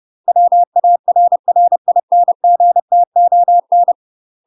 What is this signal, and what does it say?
Morse code, heard as a keyed tone, about 30 wpm, 700 Hz. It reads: WARRINGTON
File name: UK MASTER QSOs TRAINING/30WPM